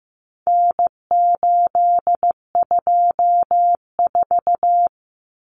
Morse code, heard as a keyed tone, about 15 wpm, 700 Hz. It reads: N824
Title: Common Cut Numbers - 4-Digit 15wpm